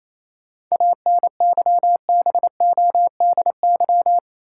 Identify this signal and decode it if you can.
Morse code, heard as a keyed tone, about 28 wpm, 700 Hz. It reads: ANYBODY
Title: Sets of 5 Words - New Word Drill - 28wpm